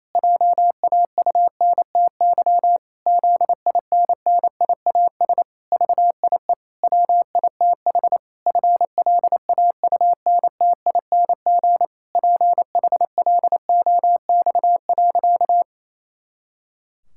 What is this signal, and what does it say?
Morse code, heard as a keyed tone, about 27 wpm, 700 Hz. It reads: JAUNTY ZINNIAS VIE WITH FLAUNTING PHLOX.
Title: Pangram Challenge v2 - 28wpm